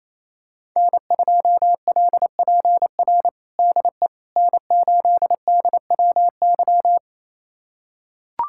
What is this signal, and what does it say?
Morse code, heard as a keyed tone, about 28 wpm, 700 Hz. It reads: N2LPR DE N8DWY
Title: USA Callsign DE Callsign - 5 Characters - 28wpm